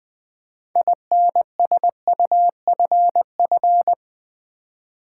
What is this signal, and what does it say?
Morse code, heard as a keyed tone, about 20 wpm, 700 Hz. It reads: INSUFF
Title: Build A Word - 6 Letters At A Time - 12-Letter Words - 20wpm